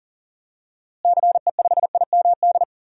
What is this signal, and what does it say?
Morse code, heard as a keyed tone, about 40 wpm, 700 Hz. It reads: CE5IMD
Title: DXCC-100 Call Signs - 6 Characters - Repeated 2x - 40wpm